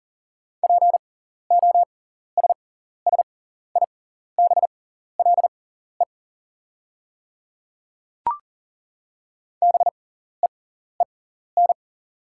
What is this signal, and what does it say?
Morse code, heard as a keyed tone, about 40 wpm, spaced as for 14 wpm, 690 Hz. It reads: POSSIBLE BEEN
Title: 6x Character Spacing - Top 500 Words - 40wpm